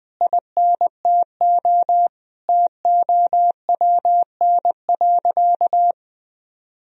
Morse code, heard as a keyed tone, about 20 wpm, 700 Hz. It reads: INTO TOWN.
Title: Sentences from Top 1000 Words - Review Part 2 - 20wpm